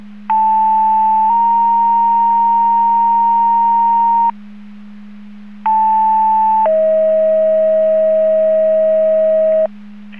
Two-tone page, heard as a one-second tone, then a three-second tone, about 880 then 930 Hz, about 880 then 640 Hz. It is heard from 0.3 to 9.7 s.